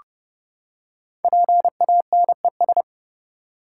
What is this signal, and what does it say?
Morse code, heard as a keyed tone, about 29 wpm, 690 Hz. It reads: PANES
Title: Code Groups - 5 Letters - 30wpm